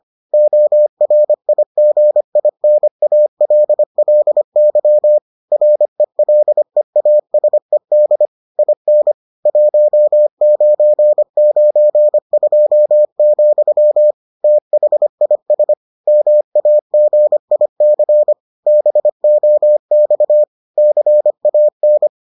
Morse code, heard as a keyed tone, about 25 wpm, 600 Hz. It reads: ORIGINALLY RELEASED IN 1992, THIS MAGIC BOX CAN